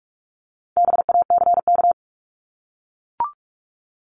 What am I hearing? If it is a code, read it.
Morse code, heard as a keyed tone, about 44 wpm, 700 Hz. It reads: BACK
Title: Top 200 Words 45wpm